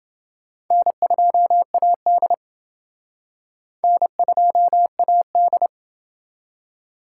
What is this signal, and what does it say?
Morse code, heard as a keyed tone, about 28 wpm, 700 Hz. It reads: N2AD N2AD